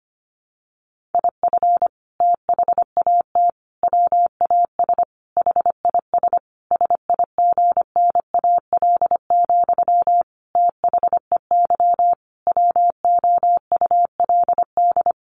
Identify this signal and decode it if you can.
Morse code, heard as a keyed tone, about 25 wpm, 700 Hz. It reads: IF THAT WAS HIS SIGNAL, THEY WOULD
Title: Sentences from Top 1400 Words 25wpm